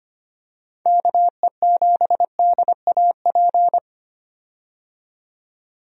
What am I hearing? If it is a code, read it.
Morse code, heard as a keyed tone, about 25 wpm, 700 Hz. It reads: KE7DAP
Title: USA Call Signs - 6 Characters - 25wpm